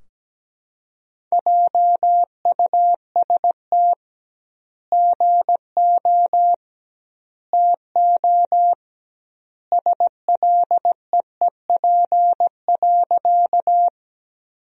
Morse code, heard as a keyed tone, about 17 wpm, 700 Hz. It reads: JUST GO TO SLEEP.
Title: 2x Word-Spacing - Sentences from Top 500 Words - 17wpm